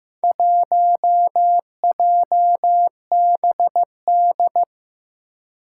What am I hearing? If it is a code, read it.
Morse code, heard as a keyed tone, about 15 wpm, 700 Hz. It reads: 1JBD
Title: Four Challenging Letter-Number Combinations 15wpm